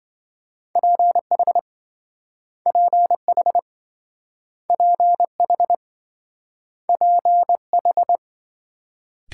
Morse code, heard as a keyed tone, about 26 wpm, 700 Hz. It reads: PH PH PH PH